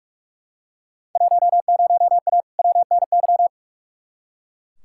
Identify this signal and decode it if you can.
Morse code, heard as a keyed tone, about 42 wpm, 700 Hz. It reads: 10A WNY
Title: Field Day Exchanges - 45wpm